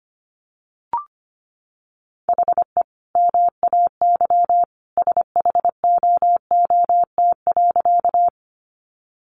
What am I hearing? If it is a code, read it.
Morse code, heard as a keyed tone, about 25 wpm, 700 Hz. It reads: HE MAY SHOOT.
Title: Sentences from Top 1200 Words 25wpm